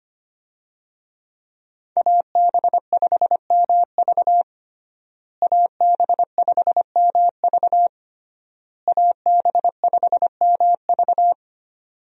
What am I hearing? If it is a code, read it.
Morse code, heard as a keyed tone, about 25 wpm, 700 Hz. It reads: AB5MV AB5MV AB5MV
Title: USA Call Signs - 5 Characters - Repeated 3x - 25wpm